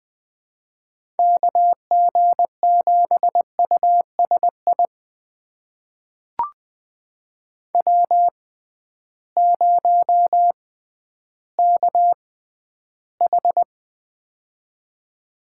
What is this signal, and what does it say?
Morse code, heard as a keyed tone, about 20 wpm, spaced as for 12 wpm, 700 Hz. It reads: KG7USI W0KH
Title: USA Call Signs - 6x Character Spacing - 20wpm